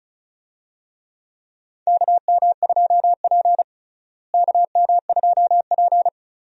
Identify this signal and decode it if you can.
Morse code, heard as a keyed tone, about 36 wpm, 700 Hz. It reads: KM2P KM2P
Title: USA Call Signs - 4 Characters - Repeated 3x - 35wpm